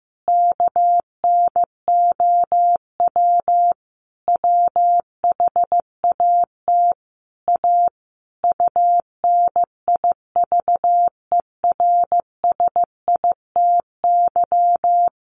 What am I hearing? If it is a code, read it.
Morse code, heard as a keyed tone, about 15 wpm, 700 Hz. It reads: KNOW WHAT A UNIVERSITY
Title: Sentences from Top 1500 Words 15wpm